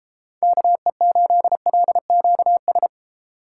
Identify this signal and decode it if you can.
Morse code, heard as a keyed tone, about 30 wpm, 700 Hz. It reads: KE8LQS